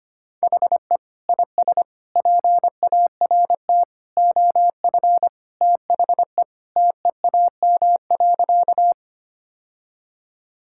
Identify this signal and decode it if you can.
Morse code, heard as a keyed tone, about 25 wpm, 700 Hz. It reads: HE IS PART OF THE TEAM.
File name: practice / 25wpm / Sentences with 3-Word Phrases - 25wpm